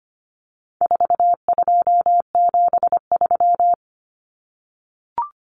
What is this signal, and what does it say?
Morse code, heard as a keyed tone, about 25 wpm, 700 Hz. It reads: 4273
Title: Numbers - 4-Digit 25wpm